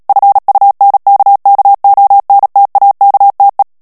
Morse code, heard as a keyed tone, about 37 wpm, 800 Hz. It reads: FUNKKONTAKTE
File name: morse mp3 / ALLEMAND texte clair 2 vitesse 35 mots minute